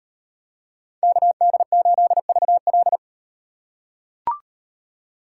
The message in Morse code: KD8VL

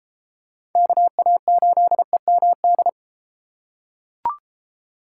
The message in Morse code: KA8EMD

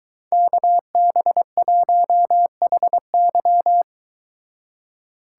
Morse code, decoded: KB1HY

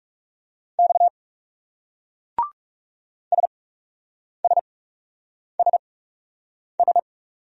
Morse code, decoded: X SSSS

